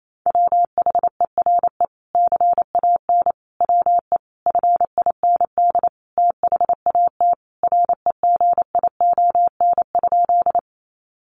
Morse code, heard as a keyed tone, about 28 wpm, 700 Hz. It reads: WHERE CAN WE FIND THAT REGION?
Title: Sentences from Top 1500 Words 28wpm